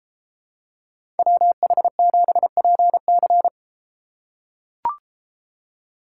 Morse code, heard as a keyed tone, about 32 wpm, 700 Hz. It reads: WH7PC